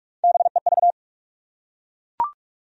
Morse code, heard as a keyed tone, about 43 wpm, 700 Hz. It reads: BEV
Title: USA Common Names - Max 5 Letters - 45wpm